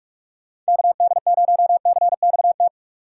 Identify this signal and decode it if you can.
Morse code, heard as a keyed tone, about 59 wpm, spaced as for 52 wpm, 700 Hz. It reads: KD0CXT
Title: Speed Racing - USA 2x3 Call Signs 45wpm